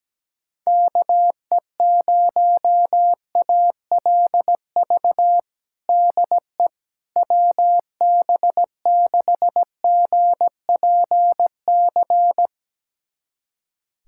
Morse code, heard as a keyed tone, about 17 wpm, 700 Hz. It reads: KE0ALV DE WB6GPC
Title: USA Callsign DE Callsign - 6 Characters - 17wpm